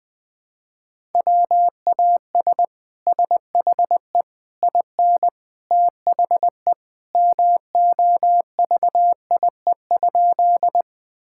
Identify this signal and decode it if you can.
Morse code, heard as a keyed tone, about 20 wpm, 700 Hz. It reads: WAS SHE IN THE MOVIE?
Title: Sentences from Top 1000 Words - Review Part 5 - 20wpm